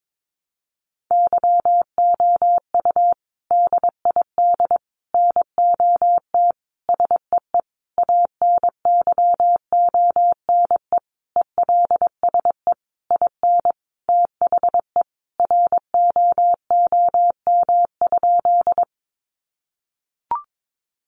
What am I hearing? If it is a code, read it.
Morse code, heard as a keyed tone, about 22 wpm, 700 Hz. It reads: YOU DID NOT SEE ANYONE ELSE IN THE ROOM?
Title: Sentences from Top 900 Words 22wpm